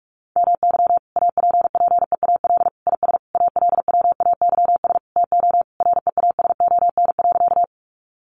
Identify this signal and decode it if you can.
Morse code, heard as a keyed tone, about 45 wpm, 700 Hz. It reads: MY APPEAL IS ALWAYS TO REASON.
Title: Sentences from Top 1300 Words 45wpm